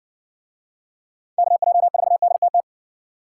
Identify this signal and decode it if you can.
Morse code, heard as a keyed tone, about 59 wpm, 700 Hz. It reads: KJ4DTT